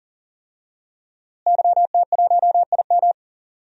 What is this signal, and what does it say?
Morse code, heard as a keyed tone, about 40 wpm, 700 Hz. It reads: YT1IM
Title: DXCC-100 Call Signs - 5 Characters - Repeated 3x - 40wpm